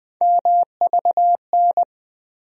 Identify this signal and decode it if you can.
Morse code, heard as a keyed tone, about 20 wpm, 700 Hz. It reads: MVN
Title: USA Call Sign Postfixes - 3 Letters 20wpm